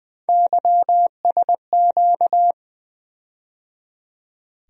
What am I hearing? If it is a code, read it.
Morse code, heard as a keyed tone, about 21 wpm, 700 Hz. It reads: YSQ
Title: USA Call Sign Postfixes - 3 Letters 20wpm